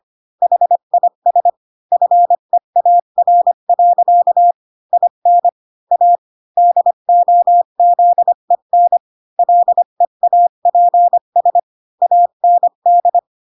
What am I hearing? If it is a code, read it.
Morse code, heard as a keyed tone, about 25 wpm, 700 Hz. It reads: HIS FEAR. IN A DOZEN LEAPS AND